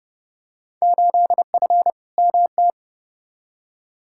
Morse code, heard as a keyed tone, about 29 wpm, 700 Hz. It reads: 8F MT
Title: Field Day Exchanges Encore - 30wpm